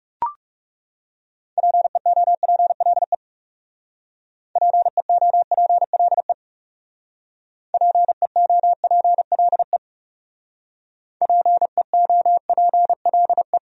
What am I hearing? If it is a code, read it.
Morse code, heard as a keyed tone, about 38 wpm, 700 Hz. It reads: PEOPLE PEOPLE PEOPLE PEOPLE